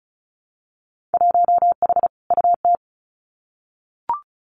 Morse code, heard as a keyed tone, about 36 wpm, 700 Hz. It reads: 1H UT
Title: Winter Field Day Exchanges 35wpm